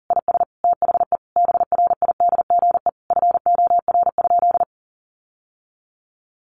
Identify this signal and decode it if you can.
Morse code, heard as a keyed tone, about 40 wpm, 700 Hz. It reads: IS THE BRIDGE FOR?
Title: Sentences from Top 1200 Words 40wpm